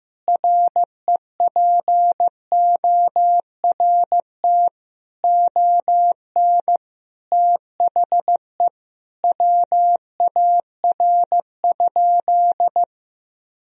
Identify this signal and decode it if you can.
Morse code, heard as a keyed tone, about 15 wpm, 700 Hz. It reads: REPORT ON THE WAR?
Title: Sentences from Top 1000 Words - Review Part 4 - 15wpm